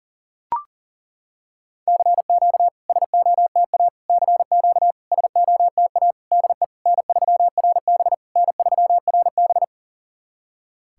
Morse code, heard as a keyed tone, about 41 wpm, 690 Hz. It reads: CQ SOTA CQ SOTA DE N3RB N3RB
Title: SOTA QSO Practice 40wpm